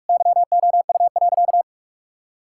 Morse code, heard as a keyed tone, about 46 wpm, 700 Hz. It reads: YOU.
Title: Sentences from Top 1000 Words - Review Part 5 - 45wpm